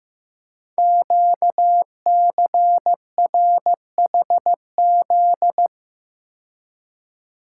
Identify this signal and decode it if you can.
Morse code, heard as a keyed tone, about 15 wpm, 700 Hz. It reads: QCRHZ
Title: Code Groups - 5 Letters - 15wpm